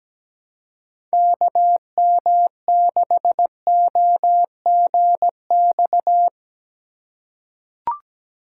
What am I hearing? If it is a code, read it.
Morse code, heard as a keyed tone, about 17 wpm, 700 Hz. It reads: KM6OGX
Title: USA Call Signs - 6 Characters - 3x Character Spacing - 17wpm